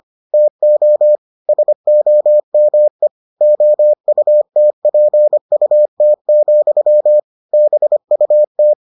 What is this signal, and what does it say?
Morse code, heard as a keyed tone, about 25 wpm, 600 Hz. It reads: TO SOME OUTPUT, BUT